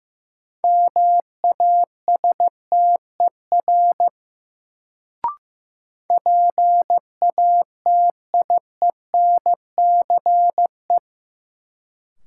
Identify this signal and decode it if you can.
Morse code, heard as a keyed tone, about 15 wpm, 700 Hz. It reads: MASTER PATIENCE
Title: Sentences from Top 1300 Words - New Word Drill Alt - 15wpm